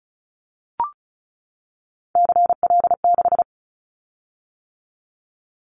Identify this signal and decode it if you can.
Morse code, heard as a keyed tone, about 34 wpm, 700 Hz. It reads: CL6
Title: Callsign Prefixes for Top 100 DXCC - 35wpm